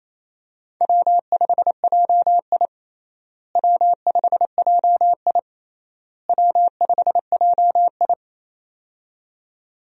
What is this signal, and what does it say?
Morse code, heard as a keyed tone, about 27 wpm, 700 Hz. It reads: W5JI W5JI W5JI
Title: USA Call Signs - 4 Characters - Repeated 3x - 28wpm